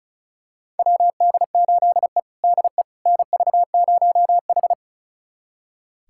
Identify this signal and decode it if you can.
Morse code, heard as a keyed tone, about 36 wpm, 700 Hz. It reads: WD8E DE NV0H
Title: USA Callsign DE Callsign - 4 Characters - 35wpm